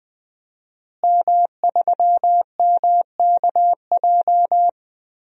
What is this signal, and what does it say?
Morse code, heard as a keyed tone, about 20 wpm, 700 Hz. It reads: M3MKJ
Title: DXCC-100 Call Signs - 5 Characters - Repeated 3x - 20wpm